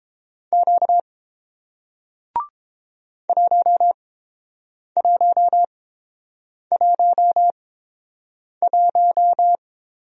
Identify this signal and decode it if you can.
Morse code, heard as a keyed tone, about 30 wpm, spaced as for 7 wpm, 700 Hz. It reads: Q 1111